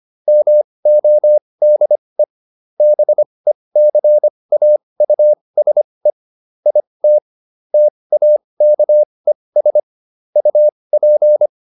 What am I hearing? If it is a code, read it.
Morse code, heard as a keyed tone, about 25 wpm, spaced as for 20 wpm, 600 Hz. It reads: MODE BECAUSE IT TAKES UP